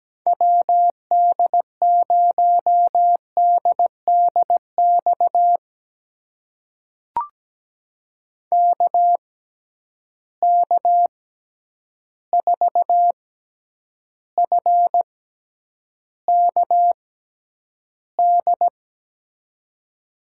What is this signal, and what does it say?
Morse code, heard as a keyed tone, about 17 wpm, spaced as for 9 wpm, 700 Hz. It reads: WD0DDX KK4FKD
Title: USA Call Signs - 6 Characters - 6x Character Spacing - 17wpm